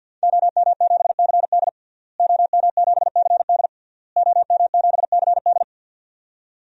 Morse code, heard as a keyed tone, about 51 wpm, 700 Hz. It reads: OM7CD OM7CD OM7CD
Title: DXCC-100 Call Signs - 5 Characters - Repeated 3x - 50wpm